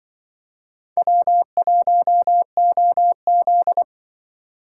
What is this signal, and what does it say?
Morse code, heard as a keyed tone, about 25 wpm, 700 Hz. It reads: W1OZ
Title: Speed Racing - USA 1x2 Call Signs 20wpm